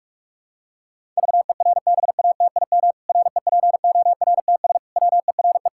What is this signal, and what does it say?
Morse code, heard as a keyed tone, about 41 wpm, 700 Hz. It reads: VERBATIM REPORTS WERE
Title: Pangram Challenge v2 - 45wpm